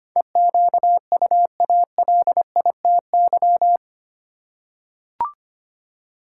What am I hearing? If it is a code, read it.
Morse code, heard as a keyed tone, about 25 wpm, 700 Hz. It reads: EQUALITY
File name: Eight Letter Words - 25wpm